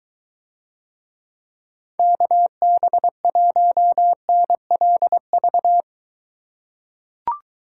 Morse code, heard as a keyed tone, about 21 wpm, 700 Hz. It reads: KB1NLV